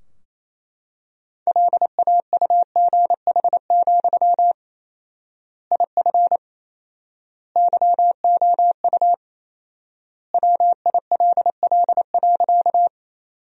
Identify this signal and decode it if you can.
Morse code, heard as a keyed tone, about 28 wpm, 700 Hz. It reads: LAUGH, IF YOU WILL.
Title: 4x Word-Spacing - Sentences from Top 500 Words - 28wpm